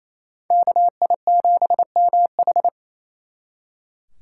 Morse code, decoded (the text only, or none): KI7MH